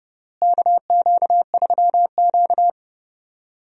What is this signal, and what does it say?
Morse code, heard as a keyed tone, about 32 wpm, 700 Hz. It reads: KQ3Q
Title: USA Call Signs - 4 Characters - Repeated 3x - 30wpm